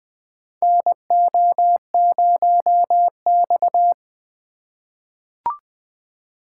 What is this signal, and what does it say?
Morse code, heard as a keyed tone, about 20 wpm, 700 Hz. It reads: NO0X
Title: USA Call Signs - 4 Characters - 4x Character Spacing - 20wpm